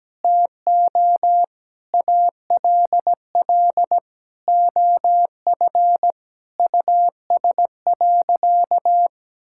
Morse code, heard as a keyed tone, about 17 wpm, 700 Hz. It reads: TO ALL OF US.
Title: Sentences from Top 1000 Words - Review Part 6 - 17wpm